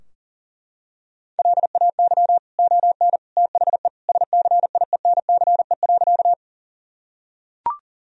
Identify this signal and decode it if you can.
Morse code, heard as a keyed tone, about 40 wpm, 700 Hz. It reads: LAY ON THE SCIENCE.
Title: Sentences from Top 1000 Words - Review Part 2 - 40wpm